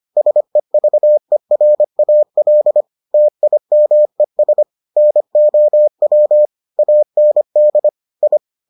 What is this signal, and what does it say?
Morse code, heard as a keyed tone, about 25 wpm, 600 Hz. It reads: SEVERAL TIMES NOW AND I